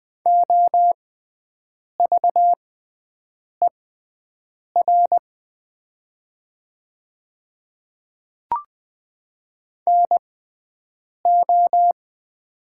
Morse code, heard as a keyed tone, about 20 wpm, spaced as for 7 wpm, 700 Hz. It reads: OVER NO
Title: 6x Character Spacing - Top 200 Words - 20wpm